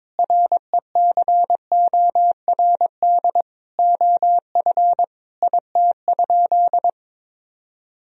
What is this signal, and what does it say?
Morse code, heard as a keyed tone, about 22 wpm, 700 Hz. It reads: RECORD OF IT?
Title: Sentences from Top 1000 Words - Review Part 3 - 22wpm